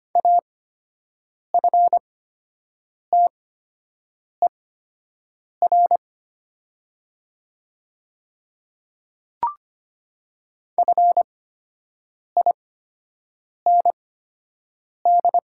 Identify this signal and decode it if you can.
Morse code, heard as a keyed tone, about 25 wpm, spaced as for 7 wpm, 700 Hz. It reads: AFTER FIND
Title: 8x Character Spacing - Top 200 Words - 25wpm